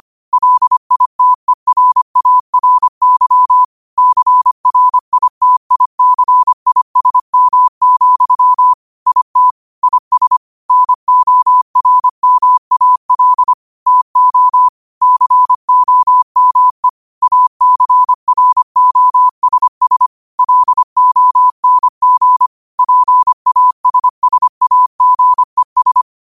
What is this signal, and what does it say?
Morse code, heard as a keyed tone, about 25 wpm, 1000 Hz. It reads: LITERARY CRITICISM, IT IS NORMAL TO COME ACROSS LONG PASSAGES